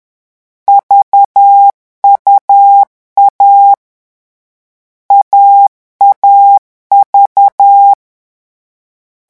Morse code, heard as a keyed tone, about 10 wpm, 800 Hz. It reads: VUA AAV
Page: Leçon 3 - vitesse 10 mots minute